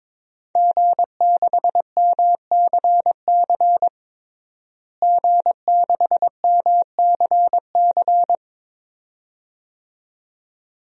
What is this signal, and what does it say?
Morse code, heard as a keyed tone, about 22 wpm, 700 Hz. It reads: G6MCC G6MCC
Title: DXCC-100 Call Signs - 5 Characters - Repeated 2x - 22wpm